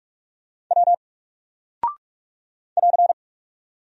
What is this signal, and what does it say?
Morse code, heard as a keyed tone, about 44 wpm, 700 Hz. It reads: W <AR>
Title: Single Character (Letter:Number:Punctuation:Prosign) 45wpm